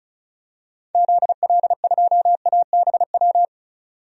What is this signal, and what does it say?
Morse code, heard as a keyed tone, about 34 wpm, 700 Hz. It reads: ZL2ABW
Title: DXCC-100 Call Signs - 6 Characters - 35wpm